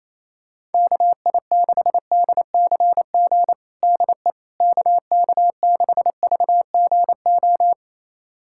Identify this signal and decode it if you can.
Morse code, heard as a keyed tone, about 27 wpm, 700 Hz. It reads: KI6DCG DE KK6VGO
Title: USA Callsign DE Callsign - 6 Characters - 28wpm